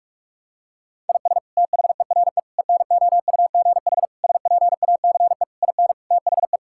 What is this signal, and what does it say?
Morse code, heard as a keyed tone, about 46 wpm, 700 Hz. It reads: IS THERE ENOUGH SPACE IN THE
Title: Sentences from Top 1000 Words - Review Part 5 - 45wpm